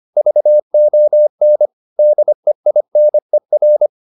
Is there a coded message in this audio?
Yes: Morse code from 0.2 to 3.9 s, heard as a keyed tone, about 25 wpm, 600 Hz. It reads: VON DEINER